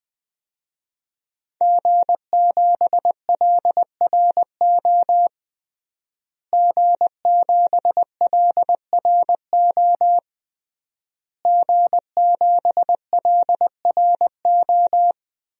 Morse code, heard as a keyed tone, about 20 wpm, 700 Hz. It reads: G7LRO G7LRO G7LRO
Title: DXCC-100 Call Signs - 5 Characters - Repeated 3x - 20wpm